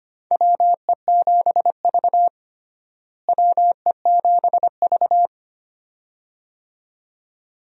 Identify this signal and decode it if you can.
Morse code, heard as a keyed tone, about 25 wpm, 700 Hz. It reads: WE7V WE7V
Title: USA Call Signs - 4 Characters - Repeated 2x - 25wpm